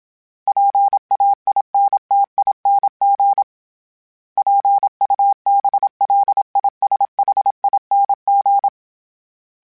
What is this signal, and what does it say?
Morse code, heard as a keyed tone, about 27 wpm, 800 Hz. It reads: PAINTING PUBLISHING
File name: Special INGs-mp3s / Special INGs-mp3s / Sngl Ing @25